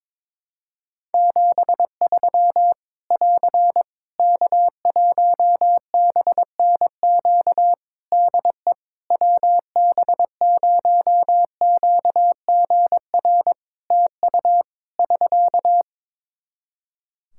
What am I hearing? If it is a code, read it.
Morse code, heard as a keyed tone, about 22 wpm, 700 Hz. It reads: 73 <AR> K1BNQ DE WB0QGR TU <SK>
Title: Practice QSO - Exchange 4 - Ending - 22wpm